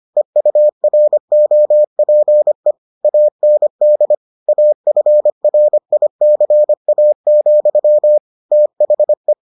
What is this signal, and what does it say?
Morse code, heard as a keyed tone, about 25 wpm, 600 Hz. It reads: EUROPE AND AFRICA, THE